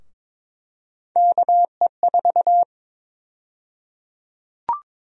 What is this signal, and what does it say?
Morse code, heard as a keyed tone, about 21 wpm, 700 Hz. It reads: KE4
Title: USA Call Sign Prefixes - Repeated 2x - 22wpm